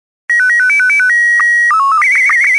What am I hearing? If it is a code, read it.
SSTV (slow-scan television), Scottie 1.